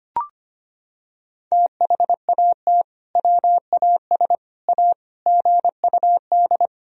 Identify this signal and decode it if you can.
Morse code, heard as a keyed tone, about 25 wpm, 700 Hz. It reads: THAT WAS A GUD
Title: Sentences from Top 200 Words and Abbreviations 25wpm